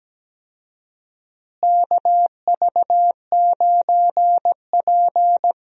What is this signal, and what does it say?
Morse code, heard as a keyed tone, about 17 wpm, 700 Hz. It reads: KV9P
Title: USA Call Signs - 4 Characters - Repeated 3x - 17wpm